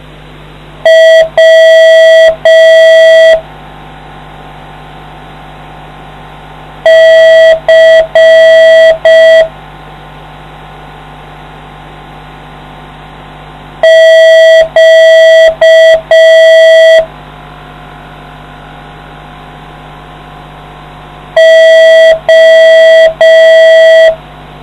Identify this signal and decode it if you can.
Morse code, heard as a keyed tone, about 5 wpm, 650 Hz. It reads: W C Q O